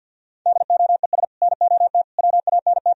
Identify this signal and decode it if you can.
Morse code, heard as a keyed tone, about 64 wpm, 700 Hz. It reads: DOES NOT WANT